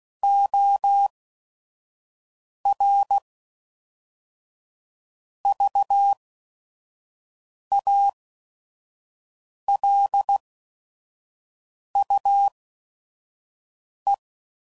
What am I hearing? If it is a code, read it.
Morse code, heard as a keyed tone, about 16 wpm, spaced as for 5 wpm, 800 Hz. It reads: OR VALUE
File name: CW Code Groups @5WPM / Technical Subjects / XL